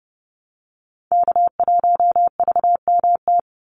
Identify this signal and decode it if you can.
Morse code, heard as a keyed tone, about 30 wpm, 700 Hz. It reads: K1VMT
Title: Call Signs Repeated 2x - 30wpm